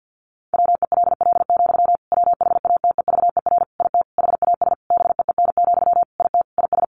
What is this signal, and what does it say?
Morse code, heard as a keyed tone, about 50 wpm, 700 Hz. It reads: WELL, WHATEVER IT HAS BEEN, IT IS
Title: Sentences from Top 1100 Words 50wpm